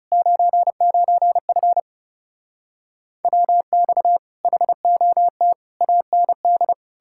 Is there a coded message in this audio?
Yes: Morse code from 0.1 to 6.7 s, heard as a keyed tone, about 40 wpm, 700 Hz. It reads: 99F WX HOT AND